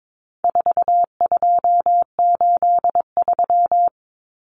Morse code, heard as a keyed tone, about 22 wpm, 700 Hz. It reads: 4283
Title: Common Cut Numbers - 4-Digit 22wpm